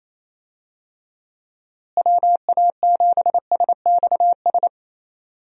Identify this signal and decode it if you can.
Morse code, heard as a keyed tone, about 28 wpm, 700 Hz. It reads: WA7SXS
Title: USA Call Signs - 6 Characters - Repeated 3x - 28wpm